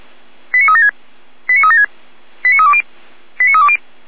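Five-tone selective call (ZVEI), heard as quick runs of five tones, tones of about 70 ms. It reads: 89376 89376 89320 89320
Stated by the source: ZVEI five-tone selective call received on an SDR.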